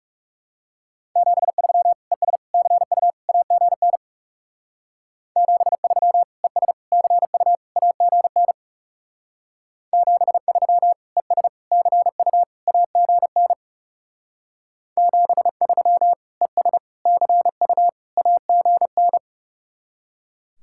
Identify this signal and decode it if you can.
Morse code, heard as a keyed tone, about 37 wpm, 700 Hz. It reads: 73 ES CU AGN 73 ES CU AGN 73 ES CU AGN 73 ES CU AGN
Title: Speed-Racing Common QSO Phrases - 30wpm